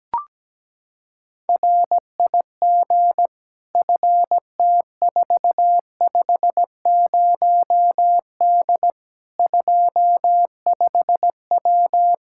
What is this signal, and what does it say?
Morse code, heard as a keyed tone, about 17 wpm, 700 Hz. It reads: RIG FT450D 25W
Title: Practice QSO - Exchange 2 - Info - 17wpm